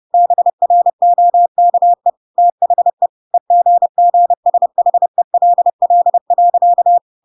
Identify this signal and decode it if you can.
Morse code, heard as a keyed tone, about 30 wpm, 700 Hz. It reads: BROKE THE EGGSHELL.
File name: Pinocchio / Pinocchio ch5 30 30 ik1hge